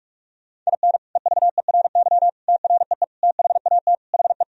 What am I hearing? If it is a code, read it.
Morse code, heard as a keyed tone, about 46 wpm, 700 Hz. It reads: IN EVERY TREE THAT HE
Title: Sentences from Top 1000 Words - Review Part 1 - 45wpm